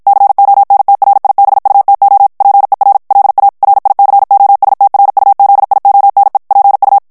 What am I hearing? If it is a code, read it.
Morse code, heard as a keyed tone, about 56 wpm, 800 Hz. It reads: CONTRIBUTO PER LA RICOSTRUZIONE PU
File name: morse mp3 / ITALIEN texte clair 1 vitesse 50 mots minute